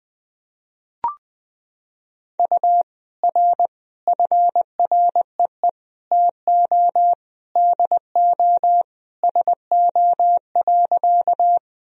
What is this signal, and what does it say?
Morse code, heard as a keyed tone, about 20 wpm, 700 Hz. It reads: U R FREE TO DO SO.
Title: Sentences from Top 500 Words and Abbreviations 20wpm